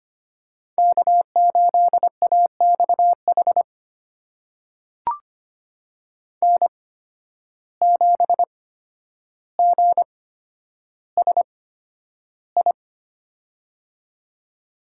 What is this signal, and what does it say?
Morse code, heard as a keyed tone, about 25 wpm, spaced as for 11 wpm, 700 Hz. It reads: K8AXH N7GSI
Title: USA Call Signs - 5 Characters - 8x Character Spacing - 25wpm